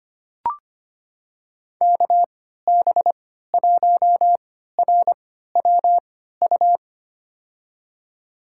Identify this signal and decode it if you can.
Morse code, heard as a keyed tone, about 25 wpm, spaced as for 14 wpm, 700 Hz. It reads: KB1RWU
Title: USA Call Signs - 6 Characters - 3x Character Spacing - 25wpm